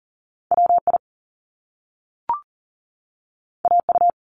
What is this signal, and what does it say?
Morse code, heard as a keyed tone, about 40 wpm, 700 Hz. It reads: WI AU
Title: Top Two Letter Combinations in Words 40wpm